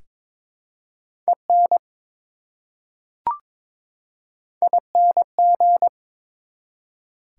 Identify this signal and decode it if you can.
Morse code, heard as a keyed tone, about 22 wpm, 700 Hz. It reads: EN ING
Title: Prefixes and Suffixes 22wpm